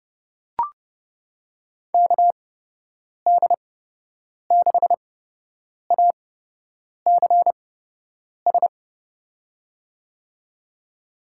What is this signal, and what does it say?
Morse code, heard as a keyed tone, about 32 wpm, spaced as for 8 wpm, 700 Hz. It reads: KD6ACS
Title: USA Call Signs - 6 Characters - 8x Character Spacing - 30wpm